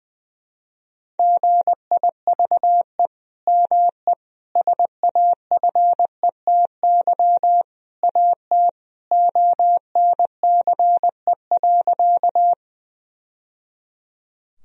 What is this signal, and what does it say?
Morse code, heard as a keyed tone, about 21 wpm, 700 Hz. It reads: GIVE ME SAFETY AT ONCE.
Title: Sentences from Top 1000 Words - Review Part 4 - 20wpm